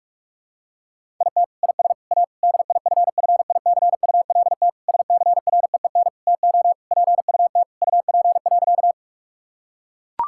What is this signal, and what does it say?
Morse code, heard as a keyed tone, about 41 wpm, 700 Hz. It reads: IT IS A DIFFICULT SCREEN TO PUT UP.
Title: Sentences from Top 1000 Words - Review Part 6 - 45wpm